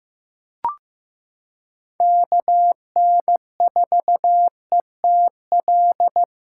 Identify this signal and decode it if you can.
Morse code, heard as a keyed tone, about 15 wpm, 700 Hz. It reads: KN4ETL
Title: USA Callsign DE Callsign 15wpm